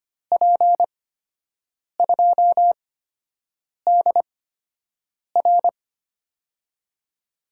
Morse code, heard as a keyed tone, about 25 wpm, spaced as for 7 wpm, 700 Hz. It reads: P2DR
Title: USA Call Signs - 5 Characters - 8x Character Spacing - 25wpm